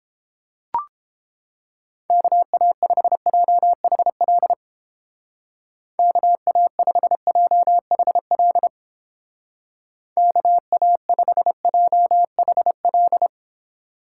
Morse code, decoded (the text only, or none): KA5JHL KA5JHL KA5JHL